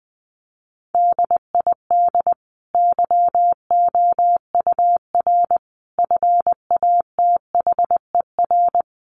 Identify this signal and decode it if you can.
Morse code, heard as a keyed tone, about 19 wpm, 700 Hz. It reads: DID YOUR FATHER
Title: Sentences from Top 1100 Words 20wpm